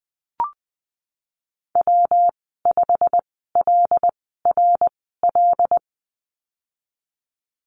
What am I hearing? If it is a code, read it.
Morse code, heard as a keyed tone, about 20 wpm, spaced as for 15 wpm, 700 Hz. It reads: W5LRL
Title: 186 Lesson 30g - Letter b - Call Signs 20wpm